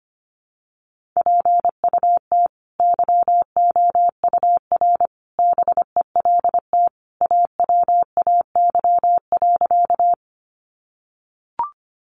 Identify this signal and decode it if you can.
Morse code, heard as a keyed tone, about 25 wpm, 700 Hz. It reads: PUT YOUR BELT AWAY.
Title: Sentences from Top 1400 Words 25wpm